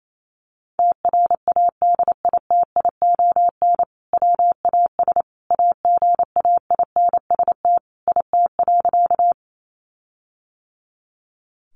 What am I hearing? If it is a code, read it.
Morse code, heard as a keyed tone, about 28 wpm, 700 Hz. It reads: TRADITION WAS AGAINST IT.
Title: Sentences from Top 1500 Words 28wpm